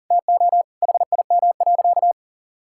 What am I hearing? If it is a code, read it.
Morse code, heard as a keyed tone, about 40 wpm, 700 Hz. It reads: TO HIM.
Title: Sentences from Top 1000 Words - Review Part 6 - 40wpm